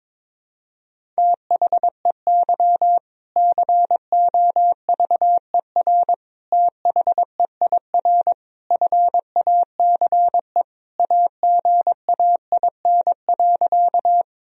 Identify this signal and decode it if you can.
Morse code, heard as a keyed tone, about 21 wpm, 700 Hz. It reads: THEY COVER THEIR FACE AGAIN.
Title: Sentences from Top 1000 Words - Review Part 1 - 22wpm